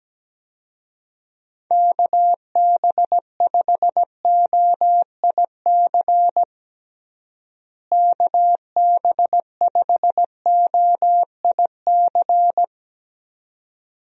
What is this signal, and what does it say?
Morse code, heard as a keyed tone, about 17 wpm, 700 Hz. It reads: KB5OIC KB5OIC
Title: USA Call Signs - 6 Characters - Repeated 2x - 17wpm